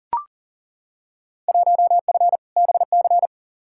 Morse code, heard as a keyed tone, about 40 wpm, 700 Hz. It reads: 1F BC
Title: Field Day Exchanges Encore - 40wpm